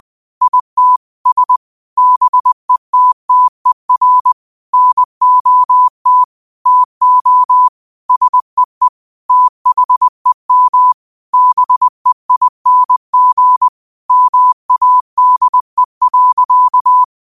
Morse code, heard as a keyed tone, about 20 wpm, 1000 Hz. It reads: IT S BETTER NOT TO SEE THEM BEING MADE.
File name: qotdfiles / 20wpm